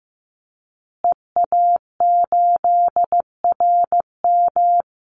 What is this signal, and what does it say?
Morse code, heard as a keyed tone, about 15 wpm, 700 Hz. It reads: EA8RM
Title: Call Signs Repeated 2x - 15wpm